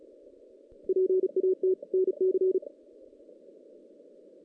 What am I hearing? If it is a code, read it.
Morse code, heard as a keyed tone, about 33 wpm, 370 Hz. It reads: PAT NC
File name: 2016 JAN CW NAQP / 2016 Jan naqp k5ot - 0-5